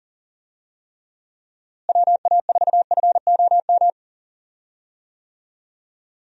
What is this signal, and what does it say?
Morse code, heard as a keyed tone, about 38 wpm, 700 Hz. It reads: WA4FOM
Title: USA Call Signs - 6 Characters - Repeated 3x - 40wpm